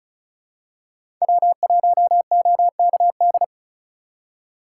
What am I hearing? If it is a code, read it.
Morse code, heard as a keyed tone, about 36 wpm, 700 Hz. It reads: W1OKD
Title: USA Call Signs - 4x Character Spacing - 35wpm